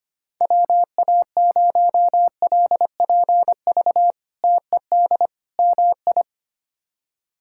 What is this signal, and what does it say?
Morse code, heard as a keyed tone, about 25 wpm, 700 Hz. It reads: WA0LPV TED MI
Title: K1USN SST QSO Practice - 25wpm